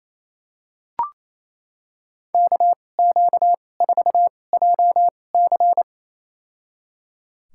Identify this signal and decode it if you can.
Morse code, heard as a keyed tone, about 28 wpm, spaced as for 20 wpm, 700 Hz. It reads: KQ4JC
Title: USA Call Signs - 5 Characters - 2x Character Spacing - 28wpm